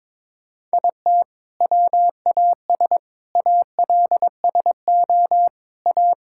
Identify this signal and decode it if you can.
Morse code, heard as a keyed tone, about 22 wpm, 700 Hz. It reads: IT WAS ALSO A
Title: Sentences from Top 1600 Words 22wpm